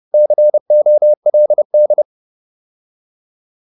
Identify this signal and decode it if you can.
Morse code, heard as a keyed tone, about 30 wpm, 600 Hz. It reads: COLD